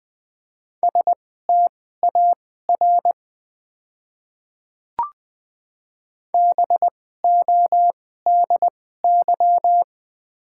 Morse code, heard as a keyed tone, about 20 wpm, spaced as for 14 wpm, 690 Hz. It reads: STAR BODY
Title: 2x Character Spacing - Top 500 Words - 20wpm